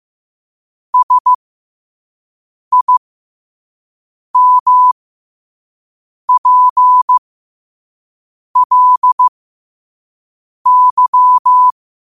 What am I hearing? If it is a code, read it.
Morse code, heard as a keyed tone, about 15 wpm, spaced as for 5 wpm, 1000 Hz. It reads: SIMPLY
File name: qotdfiles / 5wpm